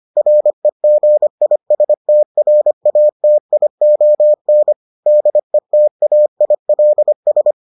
Morse code, heard as a keyed tone, about 25 wpm, 600 Hz. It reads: REGISTRATION DETAILS